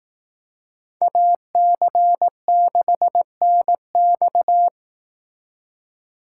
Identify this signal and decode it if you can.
Morse code, heard as a keyed tone, about 18 wpm, 700 Hz. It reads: AC6NX
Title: Speed Racing - USA 2x2 Call Signs 15wpm